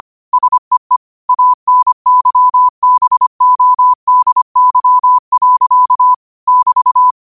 Morse code, heard as a keyed tone, about 25 wpm, 1000 Hz. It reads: SEE ANYBODY. <BT>
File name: qotdfiles / 25wpm